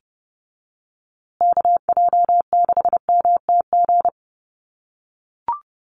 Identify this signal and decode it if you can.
Morse code, heard as a keyed tone, about 28 wpm, 700 Hz. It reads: KJ6MTG
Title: USA Call Signs Repeated 3x 30wpm